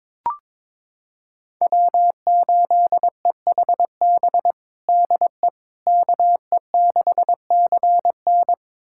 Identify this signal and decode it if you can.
Morse code, heard as a keyed tone, about 22 wpm, 700 Hz. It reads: W8EHB DE KE6CN
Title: USA Callsign DE Callsign - 5 Characters - 22wpm